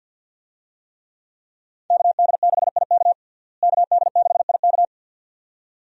Morse code, heard as a keyed tone, about 48 wpm, 700 Hz. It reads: KD6IK KD6IK
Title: USA Call Signs - 5 Characters - Repeated 2x - 50wpm